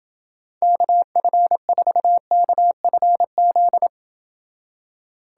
Morse code, heard as a keyed tone, about 30 wpm, 700 Hz. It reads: KF4KFZ